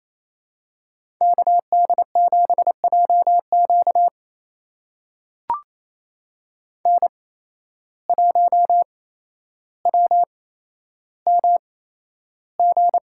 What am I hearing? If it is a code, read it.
Morse code, heard as a keyed tone, about 28 wpm, spaced as for 12 wpm, 700 Hz. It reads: KD7JQ N1WMG
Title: USA Call Signs - 5 Characters - 8x Character Spacing - 28wpm